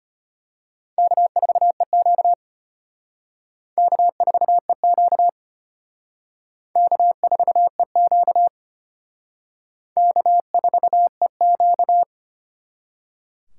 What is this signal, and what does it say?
Morse code, heard as a keyed tone, about 32 wpm, 700 Hz. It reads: K4EQ K4EQ K4EQ K4EQ